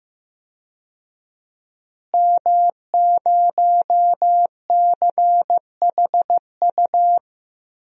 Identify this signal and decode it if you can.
Morse code, heard as a keyed tone, about 15 wpm, 700 Hz. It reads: M0CHU
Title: DXCC-100 Call Signs - 5 Characters - Repeated 2x - 15wpm